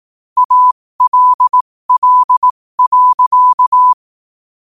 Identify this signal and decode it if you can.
Morse code, heard as a keyed tone, about 18 wpm, spaced as for 15 wpm, 1000 Hz. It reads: ALL.
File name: qotdfiles / 15wpm